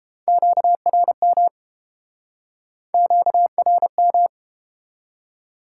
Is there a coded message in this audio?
Yes: Morse code from 0.3 to 4.3 s, heard as a keyed tone, about 31 wpm, 700 Hz. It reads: QRM QRM